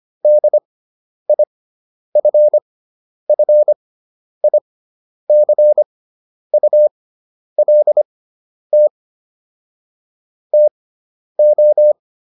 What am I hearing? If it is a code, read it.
Morse code, heard as a keyed tone, about 25 wpm, spaced as for 10 wpm, 600 Hz. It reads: DIFFICULT TO